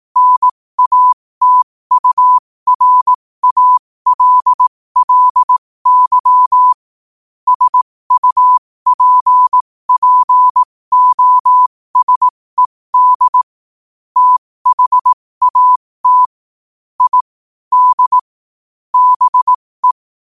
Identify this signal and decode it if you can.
Morse code, heard as a keyed tone, about 18 wpm, spaced as for 15 wpm, 1000 Hz. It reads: NATURALLY SUPPOSED THAT I D BE
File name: qotdfiles / 15wpm